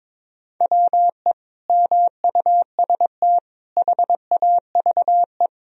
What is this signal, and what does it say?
Morse code, heard as a keyed tone, about 22 wpm, 700 Hz. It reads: WE MUST HAVE
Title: Sentences from Top 1000 Words - Review Part 2 - 22wpm